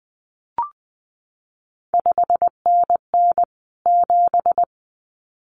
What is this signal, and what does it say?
Morse code, heard as a keyed tone, about 20 wpm, 700 Hz. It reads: 5NN 7
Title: CQ WW DX Contest Exchange Encore - 20wpm